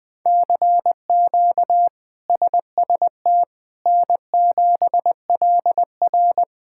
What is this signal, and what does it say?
Morse code, heard as a keyed tone, about 20 wpm, 700 Hz. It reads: CQ SST N7LR
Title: K1USN SST QSO Practice - 20wpm